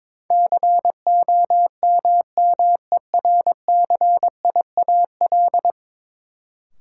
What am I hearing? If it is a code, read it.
Morse code, heard as a keyed tone, about 22 wpm, 700 Hz. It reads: COMMERCIAL
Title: Sentences from Top 1000 Words - Review Part 5 - New Word Drill - 22wpm